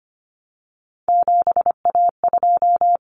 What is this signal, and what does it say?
Morse code, heard as a keyed tone, about 25 wpm, 700 Hz. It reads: 7A2
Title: Common Cut Numbers - 3-Digit 25wpm